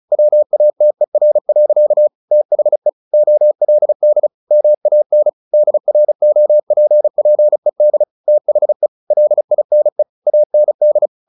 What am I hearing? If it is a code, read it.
Morse code, heard as a keyed tone, about 34 wpm, 600 Hz. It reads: WATER. THE OLD MAN DROPPED THE LINE AND
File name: the old man and the sea 2 35wpm